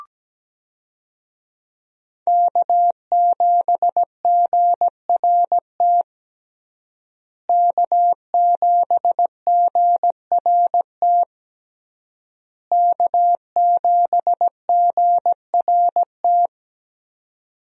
Morse code, heard as a keyed tone, about 17 wpm, 700 Hz. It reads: K7GRT K7GRT K7GRT
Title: USA Call Signs - 5 Characters - Repeated 3x - 17wpm